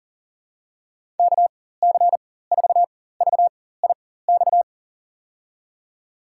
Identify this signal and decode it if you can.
Morse code, heard as a keyed tone, about 38 wpm, spaced as for 18 wpm, 700 Hz. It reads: KC4VIX
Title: USA Call Signs - 6 Characters - 4x Character Spacing - 40wpm